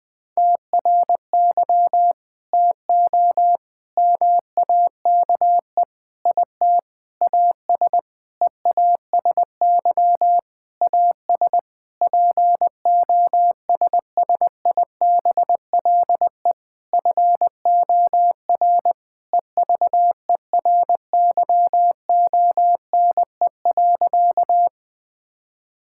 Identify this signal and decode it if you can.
Morse code, heard as a keyed tone, about 20 wpm, 700 Hz. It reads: TRY TO MAKE IT AS EASY AS POSSIBLE FOR EVERYONE.
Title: Sentences with 8-Word Phrases - Spoken Beginning - 20wpm